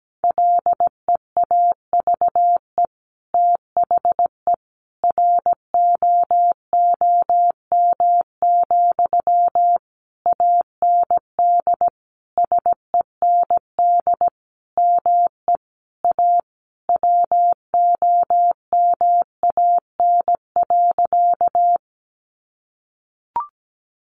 Morse code, heard as a keyed tone, about 17 wpm, 700 Hz. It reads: LEAVE THE ROOM, AND SEND ME A WOMAN.
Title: Sentences from Top 1500 Words 17wpm